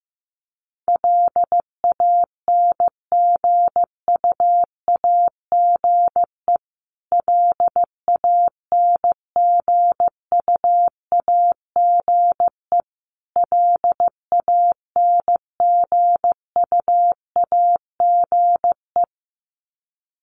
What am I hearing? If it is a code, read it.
Morse code, heard as a keyed tone, about 15 wpm, 700 Hz. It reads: LANGUAGE LANGUAGE LANGUAGE
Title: Top 500 Words Repeated 15wpm